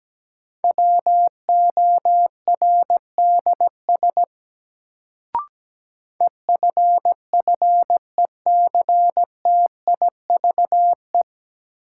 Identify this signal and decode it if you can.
Morse code, heard as a keyed tone, about 17 wpm, 700 Hz. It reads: WORDS EFFECTIVE
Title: Sentences from Top 1000 Words - Review Part 6 - New Word Drill - 17wpm